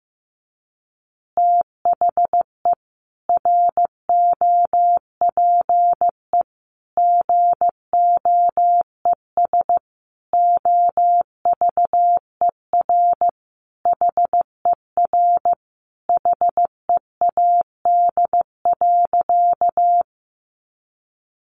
Morse code, heard as a keyed tone, about 15 wpm, 700 Hz. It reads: THE ROPE GOES OVER HER HEAD.
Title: Easy Phrases 15wpm